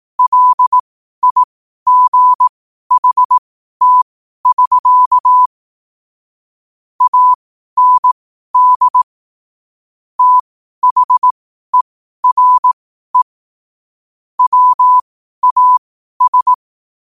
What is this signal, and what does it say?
Morse code, heard as a keyed tone, about 18 wpm, spaced as for 12 wpm, 1000 Hz. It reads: LIGHT<SK> AND THERE WAS
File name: qotdfiles / 12wpm